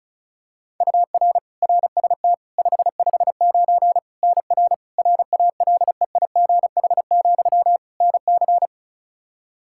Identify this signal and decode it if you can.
Morse code, heard as a keyed tone, about 34 wpm, 700 Hz. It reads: UR RST 559 NR RALEIGH, NC
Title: Practice QSO - Exchange 1 - Beginning - 35wpm